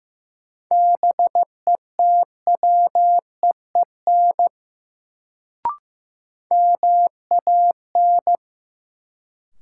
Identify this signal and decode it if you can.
Morse code, heard as a keyed tone, about 15 wpm, 700 Hz. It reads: BETWEEN MAN
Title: Sets of 2 Words - New Word Drill - 15wpm